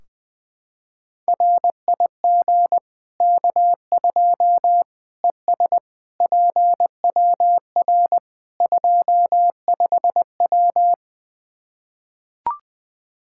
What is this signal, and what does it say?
Morse code, heard as a keyed tone, about 21 wpm, 700 Hz. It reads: RIG K2 ES PWR 25W
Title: Practice QSO - Exchange 2 - Info - 20wpm